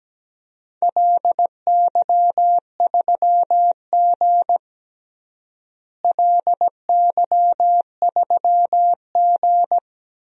Morse code, heard as a keyed tone, about 17 wpm, 700 Hz. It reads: LY3G LY3G
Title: DXCC-100 Call Signs - 4 Characters - Repeated 2x - 17wpm